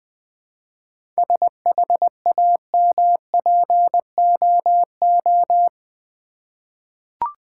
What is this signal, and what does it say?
Morse code, heard as a keyed tone, about 20 wpm, 700 Hz. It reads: SHAMPOO
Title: Easy Words with Hints - 20wpm